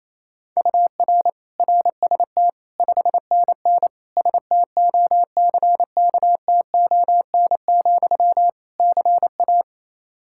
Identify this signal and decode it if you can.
Morse code, heard as a keyed tone, about 28 wpm, 700 Hz. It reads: UR RST 5NN STOCKTON, CA
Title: K1USN SST QSO Practice 28wpm